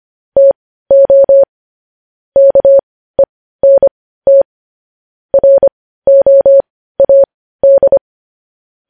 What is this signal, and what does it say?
Morse code, heard as a keyed tone, about 25 wpm, spaced as for 15 wpm, 550 Hz. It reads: TO KENT ROAD
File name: podcast / News headlines-1x-25x15